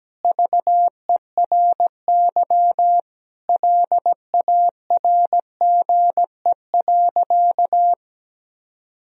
Morse code, heard as a keyed tone, about 17 wpm, 700 Hz. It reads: VERY LARGE.
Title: Sentences from Top 1000 Words - Review Part 1 - 17wpm